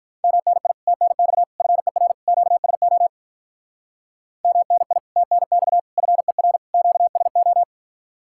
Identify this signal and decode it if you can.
Morse code, heard as a keyed tone, about 43 wpm, 700 Hz. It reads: MNI TNX FER QSO MNI TNX FER QSO
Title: Speed-Racing Common QSO Phrases - 35wpm